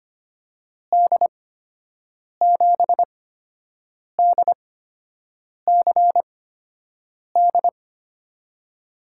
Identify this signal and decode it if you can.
Morse code, heard as a keyed tone, about 25 wpm, spaced as for 7 wpm, 700 Hz. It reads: D7DCD
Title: USA Call Signs - 6 Characters - 8x Character Spacing - 25wpm